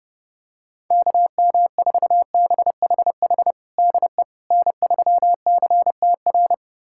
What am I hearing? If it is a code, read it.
Morse code, heard as a keyed tone, about 31 wpm, 700 Hz. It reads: KM4BHH DE N3CTR
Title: USA Callsign DE Callsign 30wpm